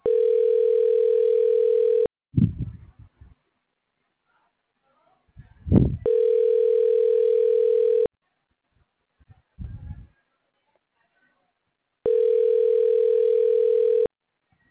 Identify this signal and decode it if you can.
Ringback tone (North America), 440 + 480 Hz.